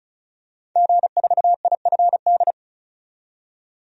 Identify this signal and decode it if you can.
Morse code, heard as a keyed tone, about 36 wpm, 700 Hz. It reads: G4IFD
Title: DXCC-100 Call Signs - 5 Characters - 35wpm